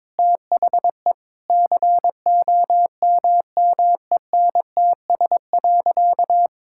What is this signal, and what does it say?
Morse code, heard as a keyed tone, about 22 wpm, 700 Hz. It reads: THE COMMENTS.
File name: practice / 22wpm / Sentences with 8-Word Phrases - 22wpm